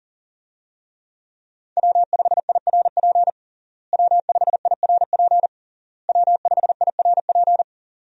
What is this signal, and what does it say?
Morse code, heard as a keyed tone, about 37 wpm, 700 Hz. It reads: W5IRP W5IRP W5IRP
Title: USA Call Signs - 5 Characters - Repeated 3x - 40wpm